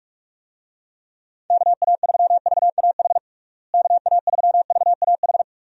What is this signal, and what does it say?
Morse code, heard as a keyed tone, about 42 wpm, 700 Hz. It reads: KA3VAH KA3VAH
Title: USA Call Signs - 6 Characters - Repeated 2x - 45wpm